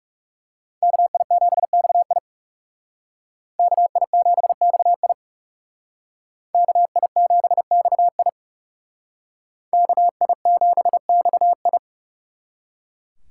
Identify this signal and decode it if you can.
Morse code, heard as a keyed tone, about 36 wpm, 700 Hz. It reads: KI7XI KI7XI KI7XI KI7XI
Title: Speed Racing - USA 2x2 Call Signs 30wpm